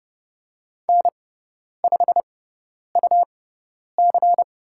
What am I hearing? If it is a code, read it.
Morse code, heard as a keyed tone, about 30 wpm, spaced as for 10 wpm, 690 Hz. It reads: N5UC
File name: Lesson 17g - Letter u - Call Signs 30wpm